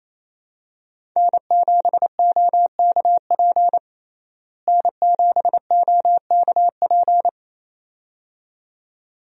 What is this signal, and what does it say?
Morse code, heard as a keyed tone, about 28 wpm, 700 Hz. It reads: N7OKP N7OKP
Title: USA Call Signs - 5 Characters - Repeated 2x - 28wpm